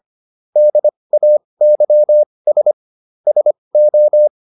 Morse code, heard as a keyed tone, about 25 wpm, spaced as for 20 wpm, 600 Hz. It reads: DAYS SO